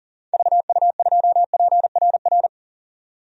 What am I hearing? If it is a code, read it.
Morse code, heard as a keyed tone, about 40 wpm, 700 Hz. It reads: VU2PRR
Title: DXCC-100 Call Signs - 6 Characters - 40wpm